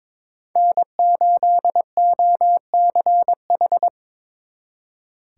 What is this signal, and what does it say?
Morse code, heard as a keyed tone, about 25 wpm, 700 Hz. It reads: N8OCH
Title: Speed Racing - USA 1x3 Call Signs 22wpm